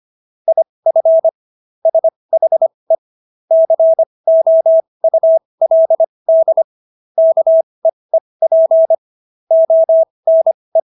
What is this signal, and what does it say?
Morse code, heard as a keyed tone, about 25 wpm, spaced as for 20 wpm, 650 Hz. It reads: IF SHE COULD KEEP ONE